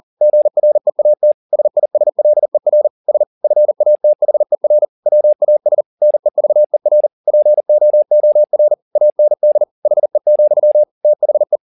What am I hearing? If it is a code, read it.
Morse code, heard as a keyed tone, about 40 wpm, 600 Hz. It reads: GREAT SISLER S FATHER WAS NEVER POOR AND HE, THE